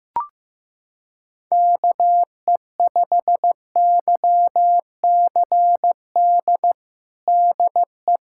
Morse code, heard as a keyed tone, about 16 wpm, 700 Hz. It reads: KE5YCD DE
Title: USA Callsign DE Callsign - 6 Characters - 15wpm